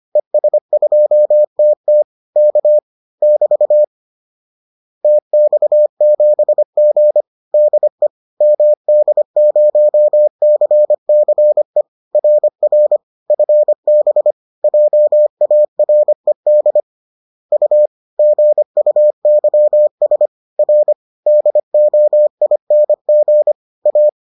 Morse code, heard as a keyed tone, about 25 wpm, 600 Hz. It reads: ES2TT K <BT> TX7G DE MD0CCE RR FB JARED U GUYS R DOING A
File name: 25wpm-6